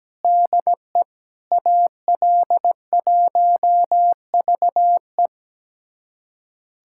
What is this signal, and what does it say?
Morse code, heard as a keyed tone, about 17 wpm, 700 Hz. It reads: DE AL1VE
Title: USA Callsign DE Callsign - 5 Characters - 17wpm